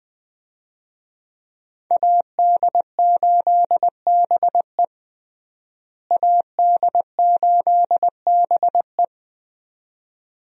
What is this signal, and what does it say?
Morse code, heard as a keyed tone, about 21 wpm, 700 Hz. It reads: AD8BE AD8BE
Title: USA Call Signs - 5 Characters - Repeated 2x - 20wpm